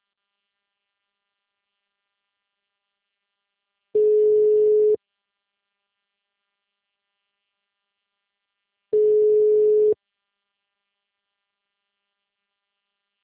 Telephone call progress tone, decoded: ringback tone

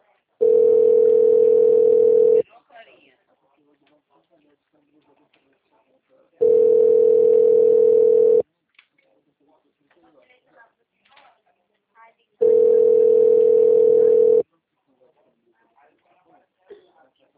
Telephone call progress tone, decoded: ringback tone

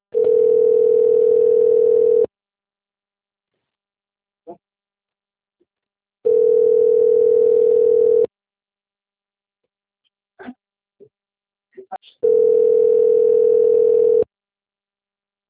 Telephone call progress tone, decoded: ringback tone

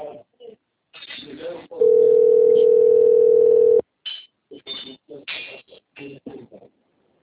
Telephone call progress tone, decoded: ringback tone

